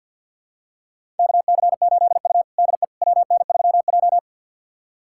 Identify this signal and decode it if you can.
Morse code, heard as a keyed tone, about 47 wpm, 700 Hz. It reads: KC8U DE WN3J
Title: USA Callsign DE Callsign - 4 Characters - 50wpm